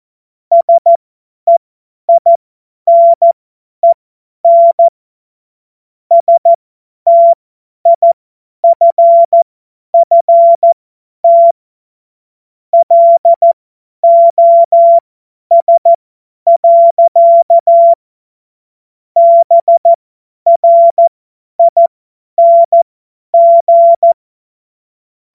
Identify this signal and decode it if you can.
Morse code, heard as a keyed tone, about 14 wpm, spaced as for 10 wpm, 680 Hz. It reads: SEINEN STIFFT LOS. BRING
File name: Morsekurs / Witze / Witz 17.Oktober 2023 50 F70